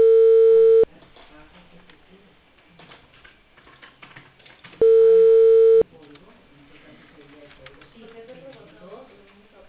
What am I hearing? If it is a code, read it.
Ringback tone (China, 450 Hz), with speech.